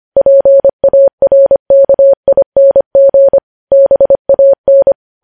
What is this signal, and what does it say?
Morse code, heard as a keyed tone, about 25 wpm, 550 Hz. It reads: PARKING BAN
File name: podcast / News headlines-1x-25x25